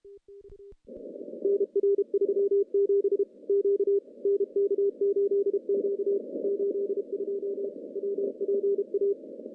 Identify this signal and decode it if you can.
Morse code, heard as a keyed tone, about 32 wpm, 400 Hz. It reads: NR 37 Q NK8Q 82 WPA
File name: contests / CW-SS-k5tr- -0000